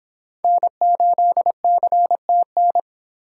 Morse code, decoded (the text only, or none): N8CTN